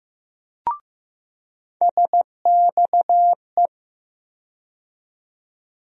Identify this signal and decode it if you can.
Morse code, heard as a keyed tone, about 15 wpm, 700 Hz. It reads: SXE
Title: USA Call Sign Postfixes - 3 Letters 15wpm